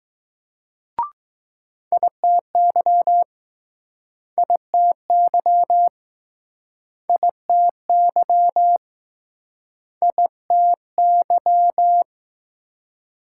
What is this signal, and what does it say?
Morse code, heard as a keyed tone, about 20 wpm, 700 Hz. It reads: ITY ITY ITY ITY